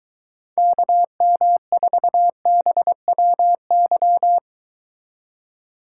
Morse code, heard as a keyed tone, about 23 wpm, 700 Hz. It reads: KM4BWY